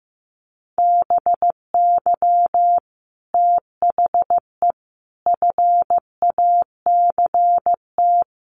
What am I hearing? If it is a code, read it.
Morse code, heard as a keyed tone, about 14 wpm, 700 Hz. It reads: BY THE FACT
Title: Sets of 5 Words Encore 15wpm